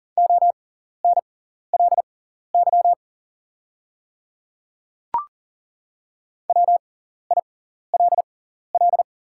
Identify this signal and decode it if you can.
Morse code, heard as a keyed tone, about 40 wpm, spaced as for 14 wpm, 690 Hz. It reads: ONLY WILL
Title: 6x Character Spacing - Top 200 Words - 40wpm